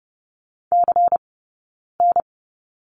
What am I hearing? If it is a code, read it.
Morse code, heard as a keyed tone, about 30 wpm, spaced as for 9 wpm, 690 Hz. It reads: CN
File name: Lesson 19d - Letter w - Two Characters 30wpm